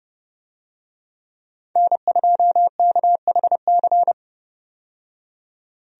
Morse code, heard as a keyed tone, about 30 wpm, 700 Hz. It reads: N2KHC
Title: USA Call Signs - 5 Characters - 30wpm